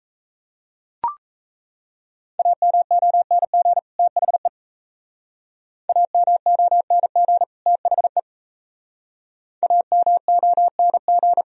Morse code, heard as a keyed tone, about 33 wpm, 700 Hz. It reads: AMONG THE AMONG THE AMONG
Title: Speed-Racing - Sets of 2 Words 28wpm